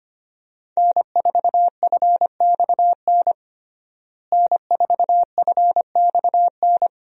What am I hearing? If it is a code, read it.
Morse code, heard as a keyed tone, about 25 wpm, 700 Hz. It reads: N4FXN N4FXN
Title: USA Call Signs - 5 Characters - Repeated 2x - 25wpm